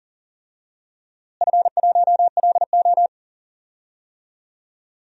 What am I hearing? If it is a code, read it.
Morse code, heard as a keyed tone, about 40 wpm, 700 Hz. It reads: F1LO
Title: DXCC-100 Call Signs - 4 Characters - Repeated 2x - 40wpm